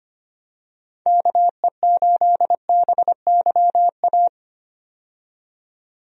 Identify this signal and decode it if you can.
Morse code, heard as a keyed tone, about 25 wpm, 700 Hz. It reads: KE8BYA
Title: USA Call Signs - 6 Characters - 25wpm